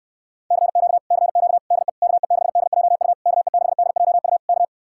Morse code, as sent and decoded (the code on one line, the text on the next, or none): -.-. --.- / -.-. --.- / -.. / -.- . -.... -.. .--- ..- / -.- . -.... -.. .--- ..- / -.-
CQ CQ DE KE6DJU KE6DJU K